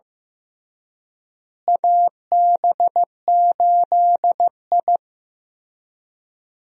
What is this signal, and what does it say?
Morse code, heard as a keyed tone, about 15 wpm, 700 Hz. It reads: AB8I
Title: USA Call Signs - 4 Characters - Repeated 2x - 15wpm